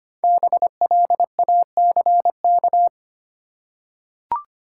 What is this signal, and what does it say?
Morse code, heard as a keyed tone, about 25 wpm, 700 Hz. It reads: BLACK
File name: 5 Letter Words - First 3 Letters Given - 25wpm